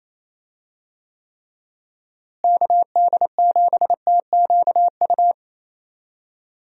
Morse code, heard as a keyed tone, about 28 wpm, 700 Hz. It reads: KD7TQU
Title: USA Call Signs - 6 Characters - Repeated 3x - 28wpm